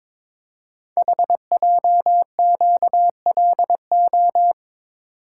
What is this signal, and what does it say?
Morse code, heard as a keyed tone, about 23 wpm, 700 Hz. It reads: HJQLO
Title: Code Groups - 5 Letters - 22wpm